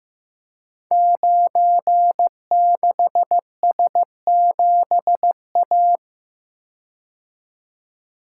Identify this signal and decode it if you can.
Morse code, heard as a keyed tone, about 15 wpm, 700 Hz. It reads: 96S7A
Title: Code Groups - 5 Letter:Numbers - 15wpm